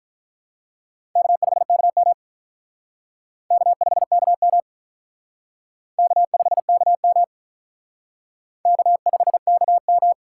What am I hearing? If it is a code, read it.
Morse code, heard as a keyed tone, about 45 wpm, 700 Hz. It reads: K5KM K5KM K5KM K5KM